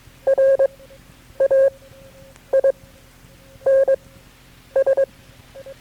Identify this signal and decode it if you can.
Morse code, heard as a keyed tone, about 21 wpm, spaced as for 8 wpm, 540 Hz. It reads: RAINS
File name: KB6MT-tape3-side1